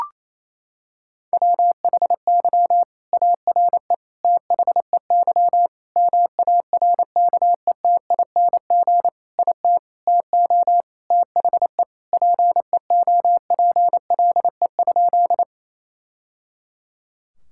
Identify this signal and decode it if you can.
Morse code, heard as a keyed tone, about 28 wpm, 700 Hz. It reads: WHY ARE THEY MARKETING IT TO THE PEOPLE?
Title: Sentences from Top 1000 Words - Review Part 5 - 28wpm